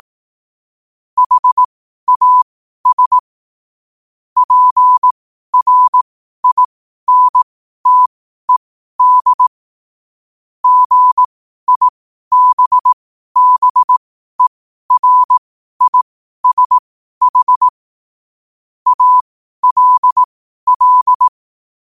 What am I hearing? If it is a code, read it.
Morse code, heard as a keyed tone, about 18 wpm, spaced as for 12 wpm, 1000 Hz. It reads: HAS PRINTED GIBBERISH ALL
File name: 12wpm